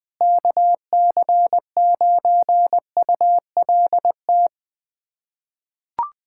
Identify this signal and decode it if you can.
Morse code, heard as a keyed tone, about 20 wpm, 700 Hz. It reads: KC9ULT
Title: USA Callsign DE Callsign - 6 Characters - 20wpm